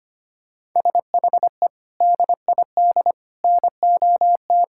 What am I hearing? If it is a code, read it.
Morse code, heard as a keyed tone, about 25 wpm, 700 Hz. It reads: SHE DID NOT
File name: practice / 25wpm / Sentences with 4-Word Phrases - 25wpm